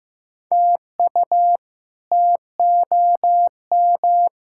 Morse code, heard as a keyed tone, about 15 wpm, 700 Hz. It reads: TU TOM
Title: K1USN SST QSO Practice - 15wpm